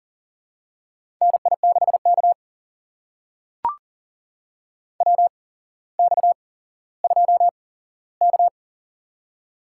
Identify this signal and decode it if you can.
Morse code, heard as a keyed tone, about 38 wpm, spaced as for 17 wpm, 700 Hz. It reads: NI6K WX2K
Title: USA Call Signs - 4 Characters - 8x Character Spacing - 40wpm